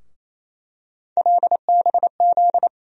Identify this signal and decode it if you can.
Morse code, heard as a keyed tone, about 28 wpm, 700 Hz. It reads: LBZ
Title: USA Call Sign Postfixes - 3 Letters 28wpm